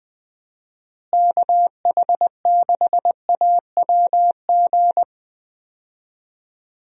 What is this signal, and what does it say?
Morse code, heard as a keyed tone, about 20 wpm, 700 Hz. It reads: KH6AWG
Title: DXCC-100 Call Signs - 6 Characters - Repeated 2x - 20wpm